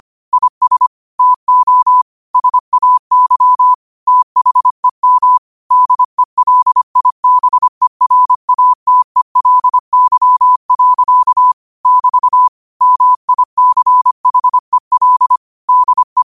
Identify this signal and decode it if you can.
Morse code, heard as a keyed tone, about 25 wpm, 1000 Hz. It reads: IS TO SAY THEM DELIBERATELY. <BT> MICHEL DE
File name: qotdfiles / 25wpm